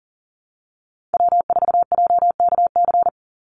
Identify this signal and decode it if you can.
Morse code, heard as a keyed tone, about 40 wpm, 700 Hz. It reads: W4JKC
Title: NA SOTA Chaser Callsigns 40wpm